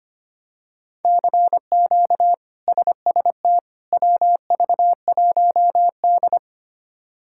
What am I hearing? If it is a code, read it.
Morse code, heard as a keyed tone, about 25 wpm, 700 Hz. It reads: CQ SST WV1D
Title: K1USN SST QSO Practice - 25wpm